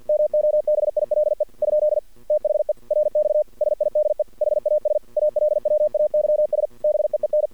Morse code, heard as a keyed tone, about 50 wpm, 600 Hz. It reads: MOBILE ? THE RX SIDE HAS ALWAYS BEEN
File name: archived / CRSnet